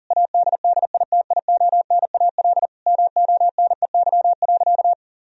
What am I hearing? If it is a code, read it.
Morse code, heard as a keyed tone, about 40 wpm, 700 Hz. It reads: ADDITIONAL MONEY.
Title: Sentences from Top 1000 Words - Review Part 4 - 40wpm